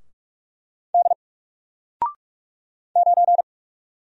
Morse code, heard as a keyed tone, about 44 wpm, 700 Hz. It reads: D 9
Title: Single Character (Letter:Number:Punctuation:Prosign) 45wpm